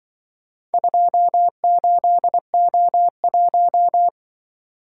Morse code, heard as a keyed tone, about 24 wpm, 700 Hz. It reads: 28O1
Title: Speed-Racing - Four Letter-Number Combinations - 20wpm